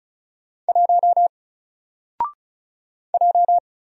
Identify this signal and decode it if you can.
Morse code, heard as a keyed tone, about 35 wpm, 700 Hz. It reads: 1 J
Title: Single Character (Letter:Number:Punctuation:Prosign) 35wpm